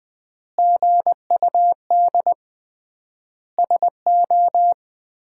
Morse code, heard as a keyed tone, about 19 wpm, 700 Hz. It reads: GUD SO
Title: Common QSO Phrases - 3x Word-Spacing - 20wpm